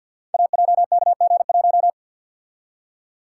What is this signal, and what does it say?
Morse code, heard as a keyed tone, about 51 wpm, 700 Hz. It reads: AJKG1
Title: Code Groups - 5 Letter:Numbers - 50wpm